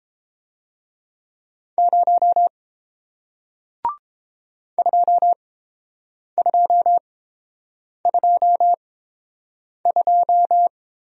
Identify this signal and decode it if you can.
Morse code, heard as a keyed tone, about 30 wpm, spaced as for 7 wpm, 700 Hz. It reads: 0 2222